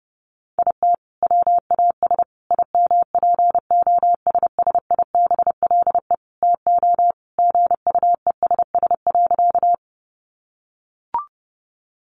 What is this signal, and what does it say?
Morse code, heard as a keyed tone, about 30 wpm, 700 Hz. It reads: IT WAS IMPOSSIBLE TO GUESS.
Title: Sentences from Top 1500 Words 30wpm